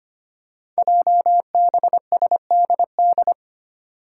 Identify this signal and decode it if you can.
Morse code, heard as a keyed tone, about 25 wpm, 700 Hz. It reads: JBSDD
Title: Five Challenging Letter-Number Combinations 25wpm